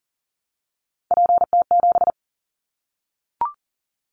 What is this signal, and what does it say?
Morse code, heard as a keyed tone, about 40 wpm, 700 Hz. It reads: PT7
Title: Callsign Prefixes for Top 50 DXCC - 40wpm